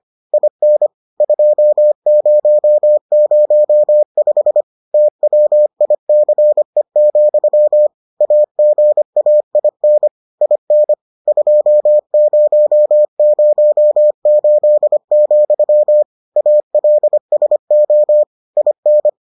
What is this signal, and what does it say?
Morse code, heard as a keyed tone, about 25 wpm, 600 Hz. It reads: IN 2005 TWICE, AGAIN IN 2008, ALSO IN